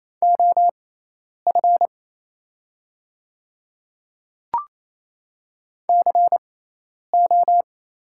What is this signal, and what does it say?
Morse code, heard as a keyed tone, about 28 wpm, spaced as for 10 wpm, 700 Hz. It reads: OF CO
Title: 6x Character Spacing - Top 500 Words - 28wpm